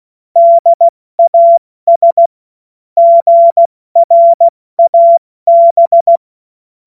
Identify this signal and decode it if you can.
Morse code, heard as a keyed tone, about 16 wpm, 680 Hz. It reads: DAS GRAB
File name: Morsekurs / Witze / Witz 06. September 2022 70 F80